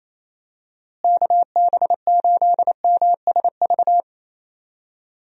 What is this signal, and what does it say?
Morse code, heard as a keyed tone, about 28 wpm, 700 Hz. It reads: KB8MSV
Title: USA Call Signs - 6 Characters - Repeated 2x - 28wpm